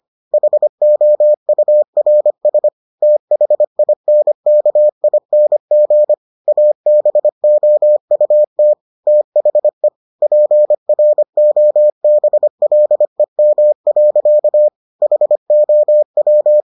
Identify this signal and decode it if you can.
Morse code, heard as a keyed tone, about 25 wpm, 600 Hz. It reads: HOURS THINKING ABOUT THE PROBLEM. HOW